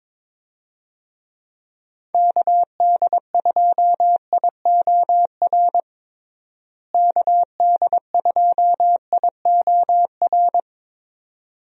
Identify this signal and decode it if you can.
Morse code, heard as a keyed tone, about 22 wpm, 700 Hz. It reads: KD2IOR KD2IOR
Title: USA Call Signs - 6 Characters - Repeated 2x - 22wpm